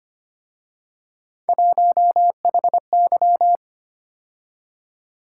Morse code, heard as a keyed tone, about 31 wpm, 700 Hz. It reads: 1HY
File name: Speed-Racing - Three Letter-Number Combinations - 25wpm